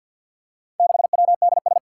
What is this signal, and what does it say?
Morse code, heard as a keyed tone, about 53 wpm, 700 Hz. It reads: 6WDS
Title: Four Challenging Letter-Number Combinations 50wpm